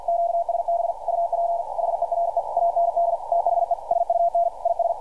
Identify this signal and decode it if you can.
Morse code, heard as a keyed tone, about 52 wpm, 680 Hz, with radio noise. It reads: QSB UP HERE AND THE SUN IS
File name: archived / CRSnet